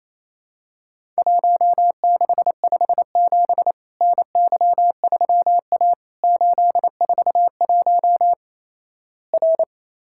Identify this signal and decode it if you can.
Morse code, heard as a keyed tone, about 28 wpm, 700 Hz. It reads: 1657 NY3A 841 R
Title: Worked All Europe DX Contest - QTC Exchange - 28wpm